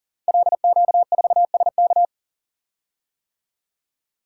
Morse code, heard as a keyed tone, about 39 wpm, 700 Hz. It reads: LQ4SK
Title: Code Groups - 5 Letter:Numbers - 40wpm